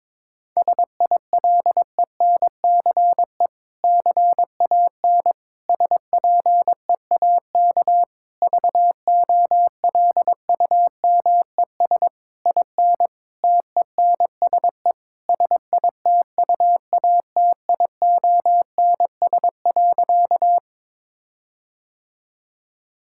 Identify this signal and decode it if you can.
Morse code, heard as a keyed tone, about 22 wpm, 700 Hz. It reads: SILENCE CAN SPEAK VOLUMES IN TENSE SITUATIONS.
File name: Difficult Sentences with Hints - 22wpm